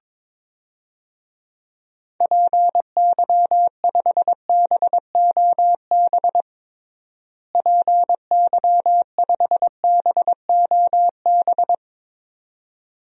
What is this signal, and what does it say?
Morse code, heard as a keyed tone, about 21 wpm, 700 Hz. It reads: PY5BOB PY5BOB
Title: DXCC-100 Call Signs - 6 Characters - Repeated 2x - 22wpm